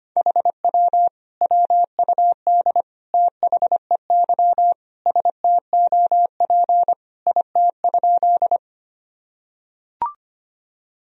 Morse code, heard as a keyed tone, about 25 wpm, 700 Hz. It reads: HW WUD THEY STOP IT?
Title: Sentences from Top 300 Words and Abbreviations 25wpm